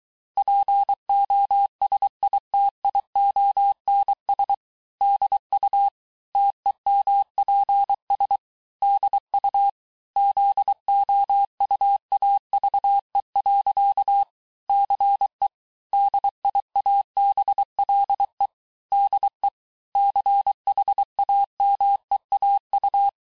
Morse code, heard as a keyed tone, about 23 wpm, 800 Hz. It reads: POSITIONS DU TEMPS DU ZOUAVE. CE DIABLE DE CHAMEAU
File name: cours-du-mercredi / audios / COURS-855--texte-clair-vitesse-22-mpm